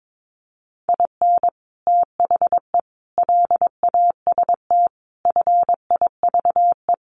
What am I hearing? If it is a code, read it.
Morse code, heard as a keyed tone, about 22 wpm, 700 Hz. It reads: IN THE LAST FIVE
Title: Sets of 5 Words Encore 22wpm